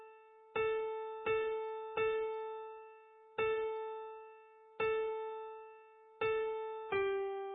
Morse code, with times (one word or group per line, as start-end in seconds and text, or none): none